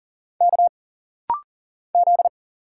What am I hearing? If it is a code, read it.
Morse code, heard as a keyed tone, about 40 wpm, 700 Hz. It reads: K Z
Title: Single Letter-Number Alt - Rapid-Fire 40wpm